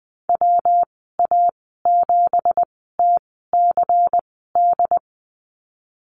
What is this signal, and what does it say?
Morse code, heard as a keyed tone, about 20 wpm, spaced as for 15 wpm, 700 Hz. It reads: WA7TCD
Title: 238 Lesson 38g - Letter q - Call Signs 20wpm